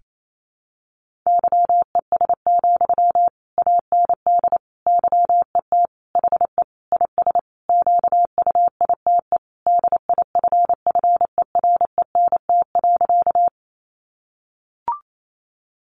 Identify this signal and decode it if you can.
Morse code, heard as a keyed tone, about 28 wpm, 700 Hz. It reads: YES, AND YET HE IS QUITE DIFFERENT.
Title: Sentences from Top 700 Words 28wpm